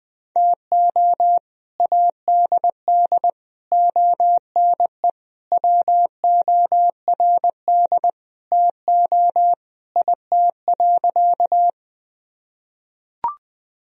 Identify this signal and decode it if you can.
Morse code, heard as a keyed tone, about 20 wpm, 700 Hz. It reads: TO ADD ONE WORD TO IT.
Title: Sentences from Top 1000 Words - Review Part 1 - 20wpm